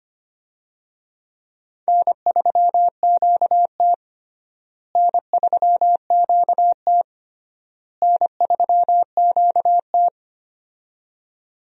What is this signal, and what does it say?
Morse code, heard as a keyed tone, about 25 wpm, 700 Hz. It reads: N3QT N3QT N3QT
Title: USA Call Signs - 4 Characters - Repeated 3x - 25wpm